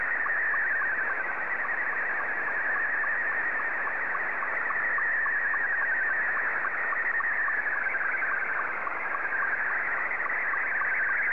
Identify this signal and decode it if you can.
SSTV (slow-scan television), Scottie 2.